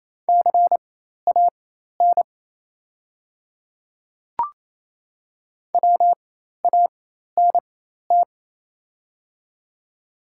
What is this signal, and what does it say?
Morse code, heard as a keyed tone, about 28 wpm, spaced as for 13 wpm, 700 Hz. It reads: CAN WANT
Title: 4x Character Spacing - Top 300 Words - 28wpm